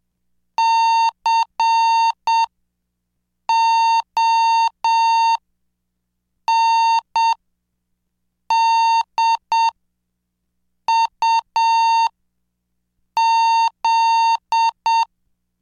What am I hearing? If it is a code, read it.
Morse code, heard as a keyed tone, about 7 wpm, spaced as for 5 wpm, 910 Hz. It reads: CONDUZ
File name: cursos / CDestudo / 26 Track 26